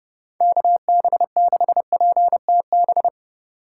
Morse code, heard as a keyed tone, about 30 wpm, 700 Hz. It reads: KB6PTB